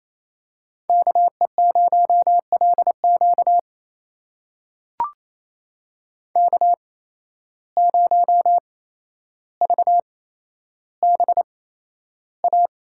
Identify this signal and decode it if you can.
Morse code, heard as a keyed tone, about 28 wpm, spaced as for 12 wpm, 700 Hz. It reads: KE0LQ K0VBA
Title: USA Call Signs - 5 Characters - 8x Character Spacing - 28wpm